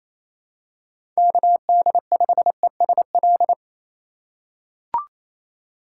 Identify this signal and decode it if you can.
Morse code, heard as a keyed tone, about 28 wpm, 700 Hz. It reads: KD5ESL
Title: USA Call Signs - 6 Characters - 3x Character Spacing - 28wpm